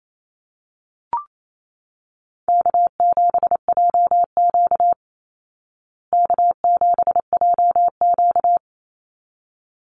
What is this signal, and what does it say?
Morse code, heard as a keyed tone, about 28 wpm, 700 Hz. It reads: K7JQ K7JQ
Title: Call Signs Repeated 3x - 28wpm